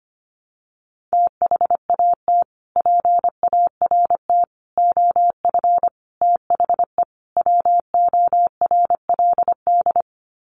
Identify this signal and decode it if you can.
Morse code, heard as a keyed tone, about 25 wpm, 700 Hz. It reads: THAT PART OF THE WORLD
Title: Sets of 5 Words Encore2 25wpm